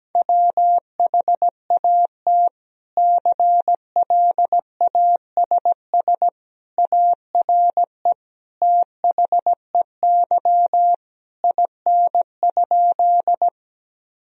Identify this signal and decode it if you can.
Morse code, heard as a keyed tone, about 17 wpm, 700 Hz. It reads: WHAT CLASS ARE THEY IN?
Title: Sentences from Top 1000 Words - Review Part 2 - 17wpm